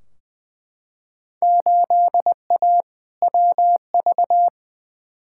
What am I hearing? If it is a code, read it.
Morse code, heard as a keyed tone, about 20 wpm, 700 Hz. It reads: 8A WV
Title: Field Day Exchanges Encore - 20wpm